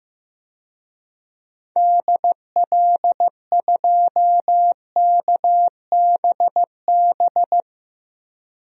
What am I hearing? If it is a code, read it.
Morse code, heard as a keyed tone, about 15 wpm, 700 Hz. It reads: DL2KBB
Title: DXCC-100 Call Signs - 6 Characters - Repeated 3x - 15wpm